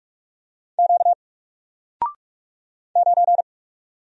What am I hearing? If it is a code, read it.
Morse code, heard as a keyed tone, about 44 wpm, 700 Hz. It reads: Q 9
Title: Single Letter-Number Alt 45wpm